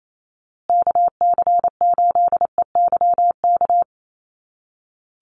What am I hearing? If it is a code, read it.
Morse code, heard as a keyed tone, about 28 wpm, 700 Hz. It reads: KC8EYK
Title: USA Call Signs Repeated 2x 28wpm